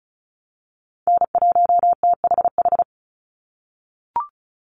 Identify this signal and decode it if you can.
Morse code, heard as a keyed tone, about 34 wpm, 700 Hz. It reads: N1THH
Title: USA Call Signs 35wpm